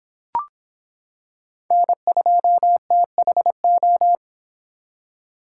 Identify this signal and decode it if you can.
Morse code, heard as a keyed tone, about 26 wpm, 700 Hz. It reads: N2THO